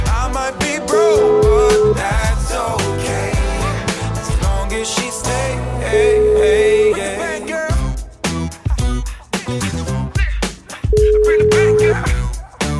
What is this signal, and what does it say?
Ringback tone (Europe, 425 Hz), with noise.